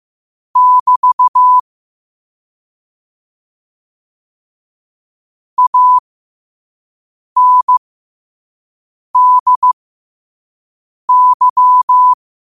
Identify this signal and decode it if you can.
Morse code, heard as a keyed tone, about 15 wpm, spaced as for 5 wpm, 1000 Hz. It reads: <BT> ANDY
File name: qotdfiles / 5wpm